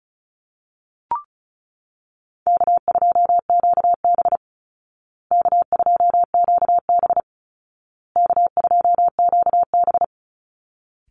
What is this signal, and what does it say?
Morse code, heard as a keyed tone, about 34 wpm, 700 Hz. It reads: K2QB K2QB K2QB
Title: Call Signs Repeated 2x - 35wpm